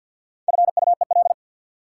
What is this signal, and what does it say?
Morse code, heard as a keyed tone, about 49 wpm, 700 Hz. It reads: FUEL
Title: Four Letter Words - 50wpm